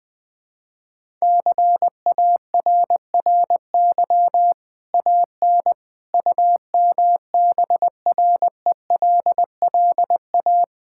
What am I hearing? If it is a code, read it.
Morse code, heard as a keyed tone, about 20 wpm, 700 Hz. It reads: CARRY AN UMBRELLA
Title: Easy Sentences with Hints - 20wpm